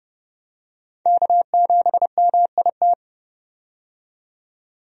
Morse code, heard as a keyed tone, about 30 wpm, 700 Hz. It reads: K7MIT
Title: USA Call Signs - 5 Characters - 30wpm